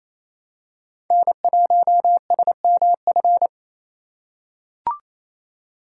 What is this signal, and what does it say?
Morse code, heard as a keyed tone, about 28 wpm, 700 Hz. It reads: N1SMF
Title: USA Call Signs - 5 Characters - 6x Character Spacing - 28wpm